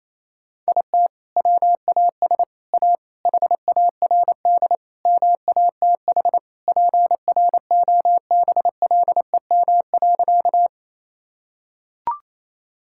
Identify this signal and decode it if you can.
Morse code, heard as a keyed tone, about 28 wpm, 700 Hz. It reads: IT WAS A HARD MATH PROBLEM.
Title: Sentences from Top 1000 Words - Review Part 6 - 28wpm